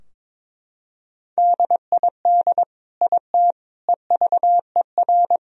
Morse code, heard as a keyed tone, about 21 wpm, 700 Hz. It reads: DID IT EVER
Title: Sentences from Top 1000 Words - Review Part 2 - 22wpm